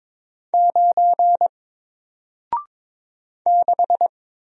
Morse code, heard as a keyed tone, about 22 wpm, 700 Hz. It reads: 9 6
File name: Single-Letter-Number-22wpm